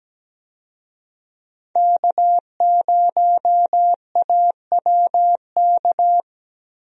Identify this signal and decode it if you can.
Morse code, heard as a keyed tone, about 17 wpm, 700 Hz. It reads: K0AWK
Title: USA Call Signs - 5 Characters - Repeated 3x - 17wpm